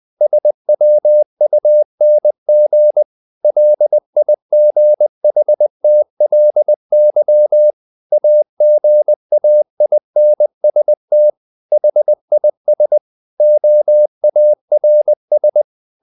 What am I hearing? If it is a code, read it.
Morse code, heard as a keyed tone, about 20 wpm, 600 Hz. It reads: SWUNG LIGHTLY AGAINST HIS OARS
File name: the old man and the sea / the old man and the sea part 1 20wpm